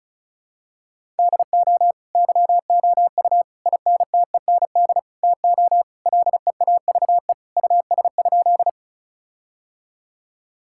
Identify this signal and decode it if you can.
Morse code, heard as a keyed tone, about 36 wpm, 700 Hz. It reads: DO YOU INTEND TO LEAVE US?
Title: Sentences from Top 1600 Words 35wpm